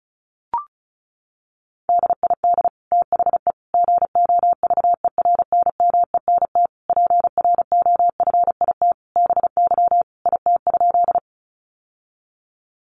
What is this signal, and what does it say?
Morse code, heard as a keyed tone, about 34 wpm, 700 Hz. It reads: DID THE GOVERNMENT PROFIT BY IT?
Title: Sentences from Top 800 Words 35wpm